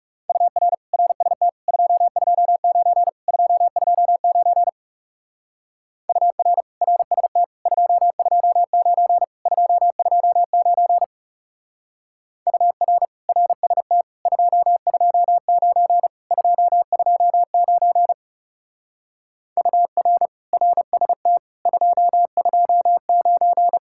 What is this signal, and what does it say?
Morse code, heard as a keyed tone, about 40 wpm, 700 Hz. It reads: UR RST 229 229 UR RST 229 229 UR RST 229 229 UR RST 229